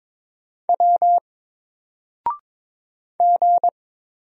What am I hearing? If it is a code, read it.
Morse code, heard as a keyed tone, about 22 wpm, 700 Hz. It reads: W G
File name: Single-Letter-Number-22wpm